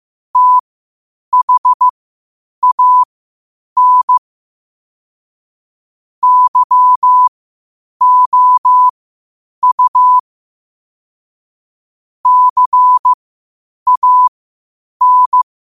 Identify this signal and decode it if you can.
Morse code, heard as a keyed tone, about 15 wpm, spaced as for 8 wpm, 1000 Hz. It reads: THAN YOU CAN
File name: qotdfiles / 8wpm